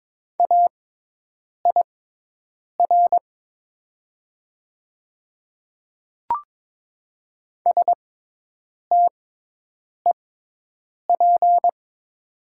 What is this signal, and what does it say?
Morse code, heard as a keyed tone, about 22 wpm, spaced as for 8 wpm, 700 Hz. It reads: AIR STEP
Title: 6x Character Spacing - Top 500 Words - 22wpm